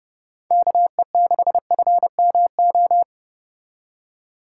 Morse code, heard as a keyed tone, about 31 wpm, 700 Hz. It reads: KE6FMO